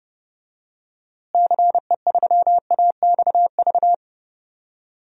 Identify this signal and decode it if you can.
Morse code, heard as a keyed tone, about 30 wpm, 700 Hz. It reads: CE3AXV
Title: DXCC-100 Call Signs - 6 Characters - Repeated 2x - 30wpm